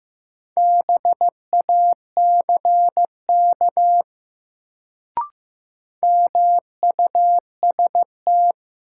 Morse code, heard as a keyed tone, about 16 wpm, 700 Hz. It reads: BACK MUST
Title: Sentences from Top 200 Words - New Word Drill - 15wpm